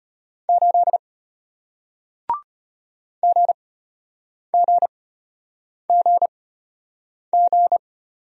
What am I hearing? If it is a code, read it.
Morse code, heard as a keyed tone, about 33 wpm, spaced as for 8 wpm, 690 Hz. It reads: 8 GGGG